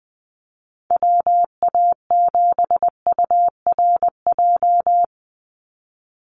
Morse code, heard as a keyed tone, about 20 wpm, 700 Hz. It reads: WA7URJ
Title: USA Call Signs 20wpm